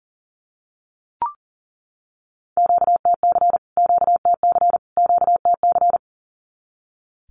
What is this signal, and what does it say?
Morse code, heard as a keyed tone, about 38 wpm, 700 Hz. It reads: QTC QTC QTC
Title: Q-Codes Repeated 3x 40wpm